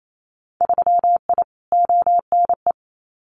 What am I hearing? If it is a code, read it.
Morse code, heard as a keyed tone, about 28 wpm, 700 Hz. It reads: 3I ONE
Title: Winter Field Day Exchanges Encore 28wpm